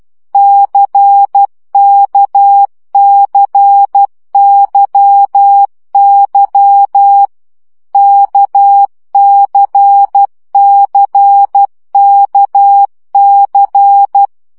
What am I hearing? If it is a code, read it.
Morse code, heard as a keyed tone, about 12 wpm, 800 Hz. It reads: CKCYY KCCKC
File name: CW gyakorlas k c y 10 baud